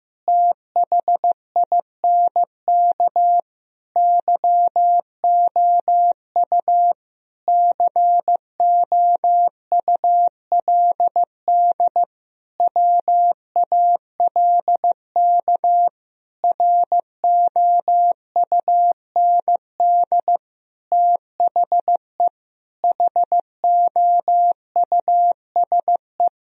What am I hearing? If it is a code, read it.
Morse code, heard as a keyed tone, about 15 wpm, 700 Hz. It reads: THINK YOU COULD WALK ROUND THE HOUSE
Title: Sentences from Top 1000 Words - Review Part 1 - 15wpm